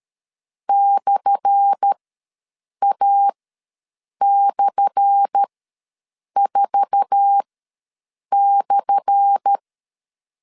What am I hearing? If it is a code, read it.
Morse code, heard as a keyed tone, about 13 wpm, spaced as for 7 wpm, 800 Hz. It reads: /A/4/
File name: LEZIONI / 10 SIGNS / 10A SLASH ws 12wpm 3443